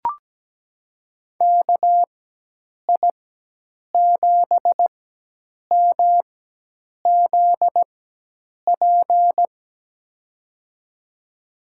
Morse code, heard as a keyed tone, about 17 wpm, spaced as for 8 wpm, 700 Hz. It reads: KI7MZP
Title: USA Call Signs - 6 Characters - 4x Character Spacing - 17wpm